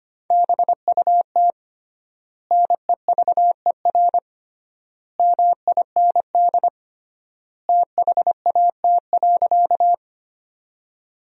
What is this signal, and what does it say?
Morse code, heard as a keyed tone, about 25 wpm, 700 Hz. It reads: BUT NEVER MIND THAT.
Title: 3x Word-Spacing - Sentences from Top 500 Words - 25wpm